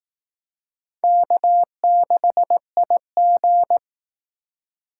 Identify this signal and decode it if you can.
Morse code, heard as a keyed tone, about 18 wpm, 700 Hz. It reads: K6IG
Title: Speed Racing - USA 1x2 Call Signs 15wpm